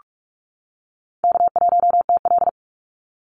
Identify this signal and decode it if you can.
Morse code, heard as a keyed tone, about 44 wpm, 700 Hz. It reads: K1TL
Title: Call Signs - 45wpm